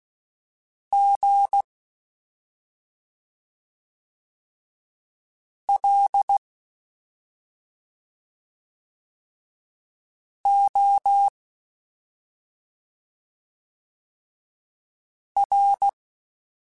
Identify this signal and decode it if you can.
Morse code, heard as a keyed tone, about 16 wpm, spaced as for 2 wpm, 800 Hz. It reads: GLOR
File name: Special INGs-mp3s / Special INGs-mp3s / Slow ing @10